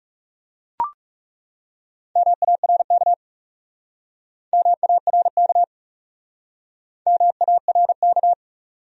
Morse code, heard as a keyed tone, about 40 wpm, 700 Hz. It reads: MARK MARK MARK